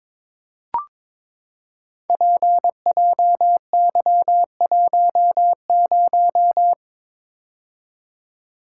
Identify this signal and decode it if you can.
Morse code, heard as a keyed tone, about 22 wpm, 700 Hz. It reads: PJY10
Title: Code Groups - 5 Letter:Numbers - 22wpm